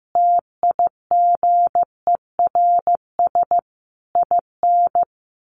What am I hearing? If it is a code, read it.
Morse code, heard as a keyed tone, about 15 wpm, 700 Hz. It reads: TIGERS IN
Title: A Sound Of Thunder 15wpm